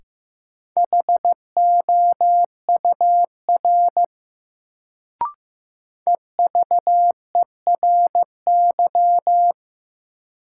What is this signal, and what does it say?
Morse code, heard as a keyed tone, about 15 wpm, 700 Hz. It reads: HOUR EVERY
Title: Sets of 5 Words - New Word Drill - 15wpm